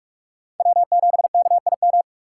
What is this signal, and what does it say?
Morse code, heard as a keyed tone, about 42 wpm, 700 Hz. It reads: W7KIM
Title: USA Call Signs - 5 Characters - 4x Character Spacing - 45wpm